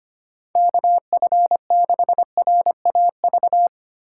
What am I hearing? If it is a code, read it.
Morse code, heard as a keyed tone, about 25 wpm, 700 Hz. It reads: KF6RAV
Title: USA Call Signs - 3x Character Spacing - 25wpm